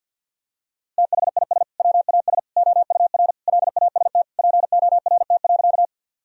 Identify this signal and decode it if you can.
Morse code, heard as a keyed tone, about 49 wpm, 700 Hz. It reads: THIS WAS OUR LAST PORT.
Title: Sentences from Top 1000 Words - Review Part 1 - 50wpm